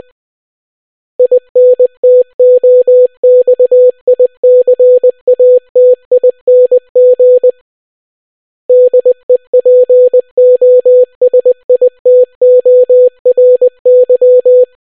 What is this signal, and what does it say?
Morse code, heard as a keyed tone, about 20 wpm, 500 Hz. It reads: INTOXICATING DEPOSITORY